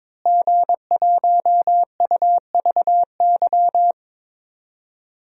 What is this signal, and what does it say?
Morse code, heard as a keyed tone, about 22 wpm, 700 Hz. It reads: G1UVY
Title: DXCC-100 Call Signs - 5 Characters - Repeated 2x - 22wpm